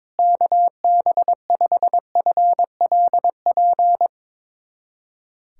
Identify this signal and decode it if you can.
Morse code, heard as a keyed tone, about 22 wpm, 700 Hz. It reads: KB5FLP
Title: USA Callsign DE Callsign - 6 Characters - 22wpm